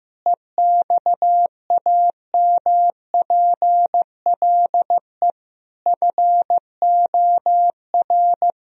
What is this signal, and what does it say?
Morse code, heard as a keyed tone, about 15 wpm, 700 Hz. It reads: EXAMPLE FOR
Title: Sentences from Top 1000 Words - Review Part 3 - 15wpm